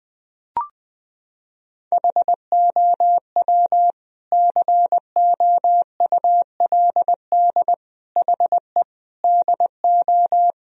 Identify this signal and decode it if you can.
Morse code, heard as a keyed tone, about 20 wpm, 700 Hz. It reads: HOW COULD HE DO
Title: Sentences from Top 1000 Words - Review Part 1 - 20wpm